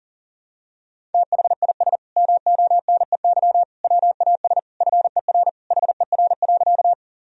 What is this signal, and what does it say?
Morse code, heard as a keyed tone, about 37 wpm, 700 Hz. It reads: THIS MONEY WAS FER HER.
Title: Sentences from Top 400 Words and Abbreviations 40wpm